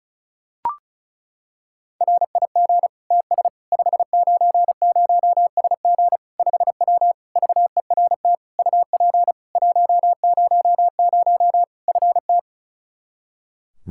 Morse code, heard as a keyed tone, about 36 wpm, 700 Hz. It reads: RIG TS 590SG 5W VERT UP 100 FT
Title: Practice QSO - Exchange 2 - Info - 35wpm